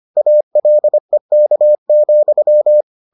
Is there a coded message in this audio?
Yes: Morse code from 0.2 to 2.8 s, heard as a keyed tone, about 25 wpm, 600 Hz. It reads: ALEK,